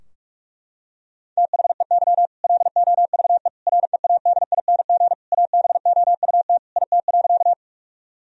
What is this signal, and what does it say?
Morse code, heard as a keyed tone, about 46 wpm, 700 Hz. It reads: THEY LOVE READING ABOUT IT.
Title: Sentences from Top 1000 Words - Review Part 3 - 45wpm